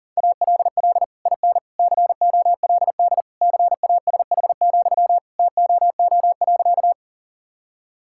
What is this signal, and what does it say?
Morse code, heard as a keyed tone, about 40 wpm, 700 Hz. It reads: ALL IN COLD CASH, TOO.
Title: Sentences from Top 1000 Words - Review Part 5 - 40wpm